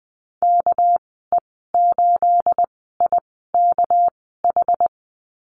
Morse code, heard as a keyed tone, about 20 wpm, spaced as for 15 wpm, 700 Hz. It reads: KE8IKH
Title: 213 Lesson 34g - Number 8 - Call Signs 20wpm